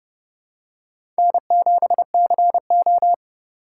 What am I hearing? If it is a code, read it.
Morse code, heard as a keyed tone, about 30 wpm, 700 Hz. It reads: N7CO
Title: USA Call Signs - 4 Characters - 4x Character Spacing - 30wpm